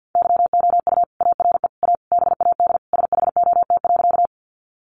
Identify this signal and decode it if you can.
Morse code, heard as a keyed tone, about 49 wpm, 700 Hz. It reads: YOU ARE A BAD SHOT.
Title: Sentences from Top 1100 Words 50wpm